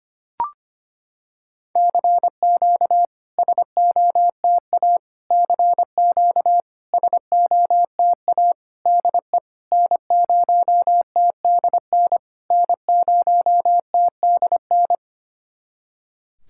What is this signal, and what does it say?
Morse code, heard as a keyed tone, about 25 wpm, 690 Hz. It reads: CQ SOTA CQ SOTA DE N0TDN N0TDN
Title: SOTA QSO Practice 25wpm